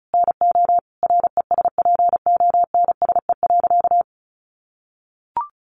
Morse code, heard as a keyed tone, about 34 wpm, 700 Hz. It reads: NO RESPONSE.
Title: Sentences from Top 1400 Words 35wpm